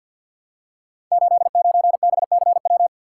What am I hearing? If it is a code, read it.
Morse code, heard as a keyed tone, about 52 wpm, 700 Hz. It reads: 89BCW
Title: Code Groups - 5 Letter:Numbers - 50wpm